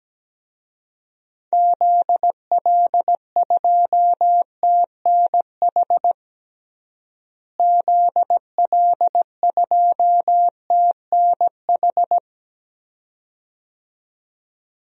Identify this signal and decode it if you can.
Morse code, heard as a keyed tone, about 17 wpm, 700 Hz. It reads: ZL2TNH ZL2TNH
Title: DXCC-100 Call Signs - 6 Characters - Repeated 2x - 17wpm